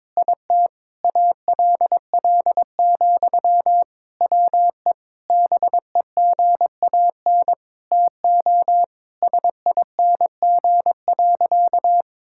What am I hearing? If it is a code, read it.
Morse code, heard as a keyed tone, about 21 wpm, 700 Hz. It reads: IT ALL, WE BEGAN TO SING.
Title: Sentences from Top 1600 Words 22wpm